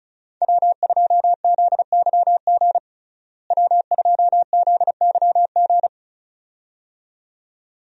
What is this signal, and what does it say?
Morse code, heard as a keyed tone, about 36 wpm, 700 Hz. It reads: W2ZYG W2ZYG
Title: USA Call Signs - 5 Characters - Repeated 2x - 35wpm